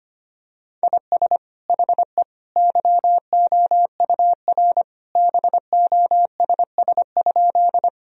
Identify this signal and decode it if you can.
Morse code, heard as a keyed tone, about 25 wpm, 700 Hz. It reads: IS HE YOUR BOSS?
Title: Sentences from Top 1000 Words - Review Part 4 - 25wpm